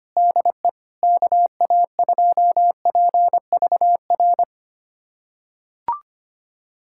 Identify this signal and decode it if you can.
Morse code, heard as a keyed tone, about 25 wpm, 700 Hz. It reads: DE KA2PVR
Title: USA Callsign DE Callsign - 6 Characters - 25wpm